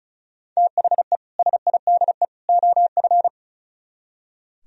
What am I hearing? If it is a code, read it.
Morse code, heard as a keyed tone, about 44 wpm, 700 Hz. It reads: THE SIDE OF
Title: Speed-Racing - Sets of 3 Words 35wpm